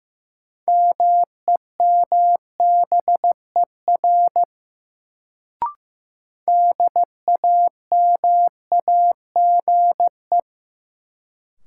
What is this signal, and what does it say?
Morse code, heard as a keyed tone, about 15 wpm, 700 Hz. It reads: MEMBER DAMAGE
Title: Sentences from Top 1100 Words - New Word Drill - 15wpm